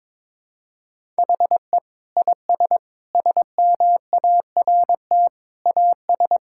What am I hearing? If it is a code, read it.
Morse code, heard as a keyed tone, about 22 wpm, 700 Hz. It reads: HE IS SMART AS
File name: practice / 22wpm / Sentences with 4-Word Phrases - 22wpm